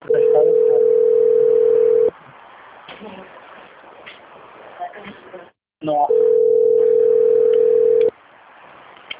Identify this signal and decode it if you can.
Ringback tone (North America), 440 + 480 Hz, with speech.
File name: out- - - - - .7463